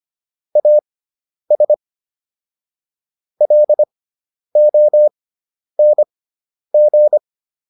Morse code, heard as a keyed tone, about 25 wpm, spaced as for 10 wpm, 600 Hz. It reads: AS LONG